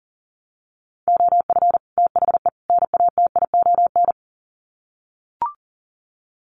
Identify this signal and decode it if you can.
Morse code, heard as a keyed tone, about 40 wpm, 700 Hz. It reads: OF THE NATION
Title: Sets of 3 Words 40wpm